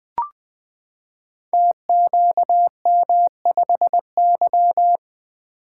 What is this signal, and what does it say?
Morse code, heard as a keyed tone, about 21 wpm, 700 Hz. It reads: TQM5Y
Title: Code Groups - 5 Letter:Numbers - 20wpm